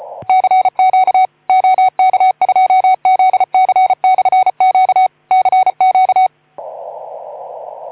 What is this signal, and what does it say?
Morse code, heard as a keyed tone, about 34 wpm, 750 Hz. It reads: CQ OK2ZC/Q CQ
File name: Archiv / CUC181 / OK2ZC CUC 181